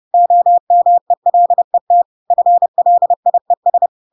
Morse code, heard as a keyed tone, about 30 wpm, 700 Hz. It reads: OMELET FLIES